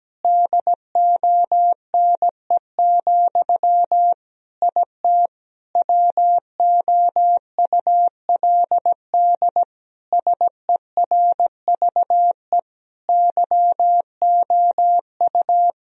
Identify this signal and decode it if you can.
Morse code, heard as a keyed tone, about 17 wpm, 700 Hz. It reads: DONE, IT WOULD SERVE YOU
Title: Sentences from Top 1000 Words - Review Part 2 - 17wpm